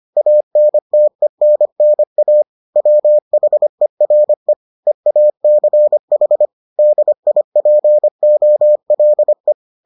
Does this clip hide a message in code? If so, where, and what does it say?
Yes: Morse code from 0.2 to 9.5 s, heard as a keyed tone, about 25 wpm, 600 Hz. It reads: ANTENNA WHERE EACH DIPOLE